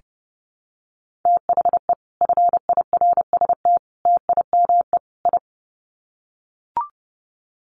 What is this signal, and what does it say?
Morse code, heard as a keyed tone, about 30 wpm, 700 Hz. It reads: THE FIRST TIME I
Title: Sets of 4 Words 30wpm